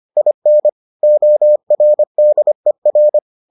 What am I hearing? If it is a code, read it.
Morse code, heard as a keyed tone, about 25 wpm, 600 Hz. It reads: IN ORDER